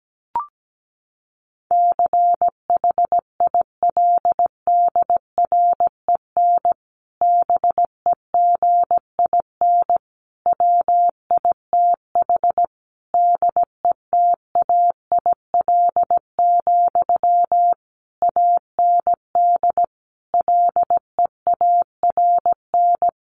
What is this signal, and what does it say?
Morse code, heard as a keyed tone, about 17 wpm, 700 Hz. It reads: CHILDREN BEGIN WITH DETAIL, AND LEARN
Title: Sentences from Top 1500 Words 17wpm